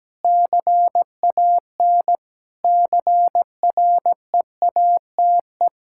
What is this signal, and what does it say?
Morse code, heard as a keyed tone, about 17 wpm, 700 Hz. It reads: CAN CREATE
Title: Pangram Challenge v2 - 17wpm